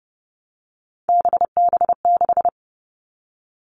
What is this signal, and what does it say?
Morse code, heard as a keyed tone, about 30 wpm, 700 Hz. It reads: BB6
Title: Three Challenging Letter-Number Combinations 30wpm